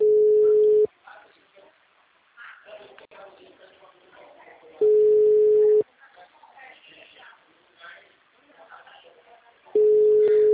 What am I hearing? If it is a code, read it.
Ringback tone (Europe, 425 Hz), with speech.